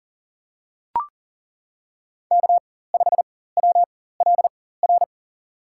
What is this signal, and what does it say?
Morse code, heard as a keyed tone, about 40 wpm, spaced as for 19 wpm, 700 Hz. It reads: K5WLR
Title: USA Call Signs - 5 Characters - 4x Character Spacing - 40wpm